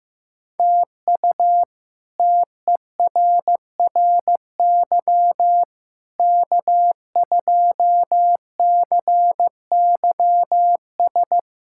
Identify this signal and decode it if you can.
Morse code, heard as a keyed tone, about 15 wpm, 700 Hz. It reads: TU TERRY K2CYS
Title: K1USN SST QSO Practice - 15wpm